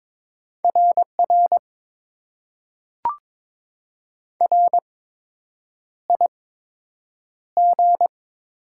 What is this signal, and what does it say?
Morse code, heard as a keyed tone, about 22 wpm, 700 Hz. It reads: R R R I G
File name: Top 90 QSO Elements - 8x Character Spacing - 22wpm